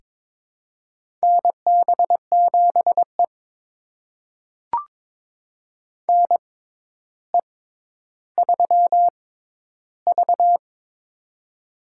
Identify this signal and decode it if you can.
Morse code, heard as a keyed tone, about 23 wpm, spaced as for 11 wpm, 700 Hz. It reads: NB7E NE3V
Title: USA Call Signs - 4 Characters - 6x Character Spacing - 22wpm